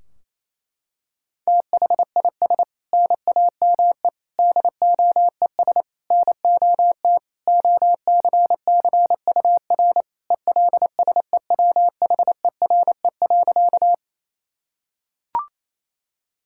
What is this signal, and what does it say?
Morse code, heard as a keyed tone, about 28 wpm, 700 Hz. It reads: THIS NAME DOES NOT OCCUR ELSEWHERE.
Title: Sentences from Top 1600 Words 28wpm